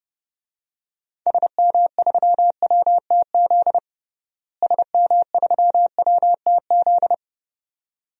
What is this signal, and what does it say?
Morse code, heard as a keyed tone, about 30 wpm, 700 Hz. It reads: SM3WTZ SM3WTZ
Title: DXCC-100 Call Signs - 6 Characters - Repeated 2x - 30wpm